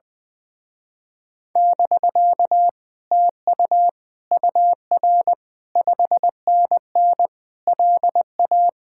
Morse code, heard as a keyed tone, about 20 wpm, 700 Hz. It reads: <BK> TU UR 5NN LA
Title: Speed-Racing Common QSO Phrases - 20wpm